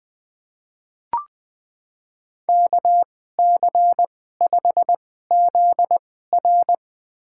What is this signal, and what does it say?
Morse code, heard as a keyed tone, about 20 wpm, spaced as for 14 wpm, 700 Hz. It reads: KC5ZR
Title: USA Call Signs - 2x Character Spacing - 20wpm